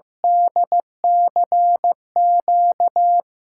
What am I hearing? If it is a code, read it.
Morse code, heard as a keyed tone, about 16 wpm, 700 Hz. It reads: DCQ
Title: USA Call Signs - 5 Characters - 6x Character Spacing - 15wpm